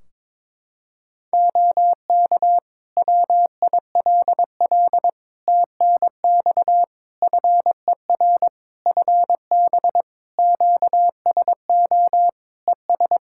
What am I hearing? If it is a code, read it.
Morse code, heard as a keyed tone, about 22 wpm, 700 Hz. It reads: OK WILL TNX FER FB QSO ES
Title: Practice QSO - Exchange 4 - Ending - 22wpm